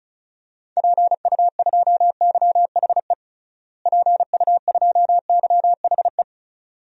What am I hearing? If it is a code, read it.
Morse code, heard as a keyed tone, about 34 wpm, 700 Hz. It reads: PU2YHE PU2YHE
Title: DXCC-100 Call Signs - 6 Characters - Repeated 2x - 35wpm